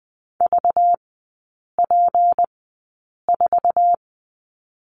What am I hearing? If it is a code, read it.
Morse code, heard as a keyed tone, about 20 wpm, spaced as for 8 wpm, 690 Hz. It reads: VP4
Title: 198 Lesson 32e - Letter k - Three Characters 20wpm